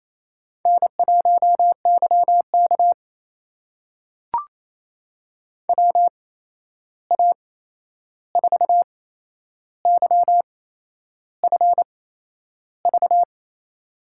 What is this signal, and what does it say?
Morse code, heard as a keyed tone, about 28 wpm, spaced as for 11 wpm, 700 Hz. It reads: N1YK WA4YFV
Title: USA Call Signs - 8x Character Spacing - 28wpm